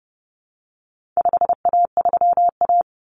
Morse code, heard as a keyed tone, about 29 wpm, 700 Hz. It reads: 5A3A
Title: Common Cut Numbers - 4-Digit 30wpm